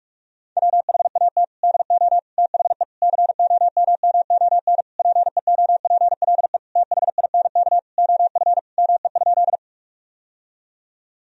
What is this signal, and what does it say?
Morse code, heard as a keyed tone, about 46 wpm, 700 Hz. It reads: WHAT DO THE COMMON PEOPLE THINK OF ME?
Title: Sentences from Top 1000 Words - Review Part 3 - 45wpm